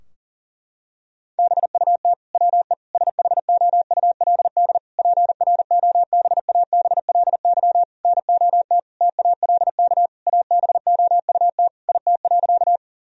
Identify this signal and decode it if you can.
Morse code, heard as a keyed tone, about 40 wpm, 700 Hz. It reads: BUT WE SHOULD PROBABLY NOT TALK ABOUT IT.
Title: Sentences from Top 1000 Words - Review Part 4 - 40wpm